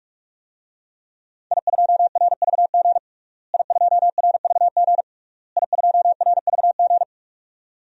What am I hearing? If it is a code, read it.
Morse code, heard as a keyed tone, about 47 wpm, 700 Hz. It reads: I2RVG I2RVG I2RVG
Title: DXCC-100 Call Signs - 5 Characters - Repeated 3x - 45wpm